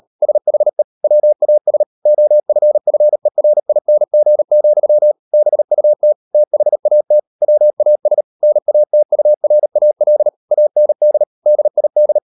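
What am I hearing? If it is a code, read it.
Morse code, heard as a keyed tone, about 38 wpm, 600 Hz. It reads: SHE WAS OFFERING, BUT THAT WAS NATURAL AND DID
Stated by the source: Morse code practice at 38 WPM, sending Ken Follett's novel Never in English, from CW med Gnister.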